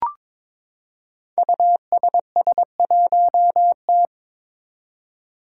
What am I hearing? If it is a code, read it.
Morse code, heard as a keyed tone, about 22 wpm, 700 Hz. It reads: USS1T
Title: Five Challenging Letter-Number Combinations 22wpm